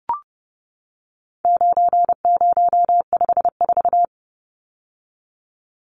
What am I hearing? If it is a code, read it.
Morse code, heard as a keyed tone, about 30 wpm, 700 Hz. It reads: 9054
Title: Numbers - 4-Digit 30wpm